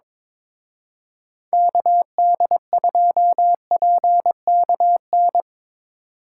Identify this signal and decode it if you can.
Morse code, heard as a keyed tone, about 25 wpm, 700 Hz. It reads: KD2PKN
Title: Speed Racing - USA 2x3 Call Signs 22wpm